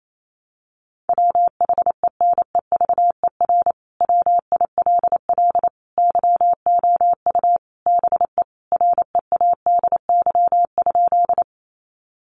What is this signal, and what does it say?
Morse code, heard as a keyed tone, about 28 wpm, 700 Hz. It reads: WHENEVER WILL YOU BE READY?
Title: Sentences from Top 1200 Words 28wpm